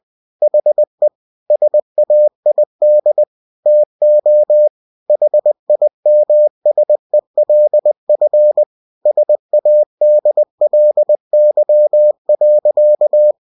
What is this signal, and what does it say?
Morse code, heard as a keyed tone, about 20 wpm, 600 Hz. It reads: HE SAID TO HIMSELF SADLY.